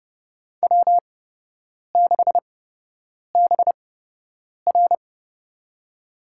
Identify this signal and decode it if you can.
Morse code, heard as a keyed tone, about 30 wpm, spaced as for 8 wpm, 700 Hz. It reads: W6BR
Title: USA Call Signs - 4 Characters - 8x Character Spacing - 30wpm